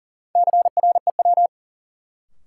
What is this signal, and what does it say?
Morse code, heard as a keyed tone, about 40 wpm, 700 Hz. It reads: CREW
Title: Sentences from Top 1500 Words - New Word Drill Alt - 40wpm